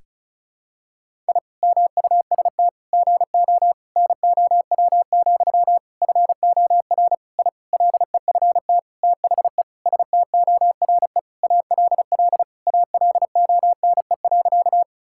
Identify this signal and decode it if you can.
Morse code, heard as a keyed tone, about 36 wpm, 700 Hz. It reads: I MUST GO NOW, FOR I LEFT THE STORE ALL ALONE.
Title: Sentences from Top 1000 Words - Review Part 5 - 35wpm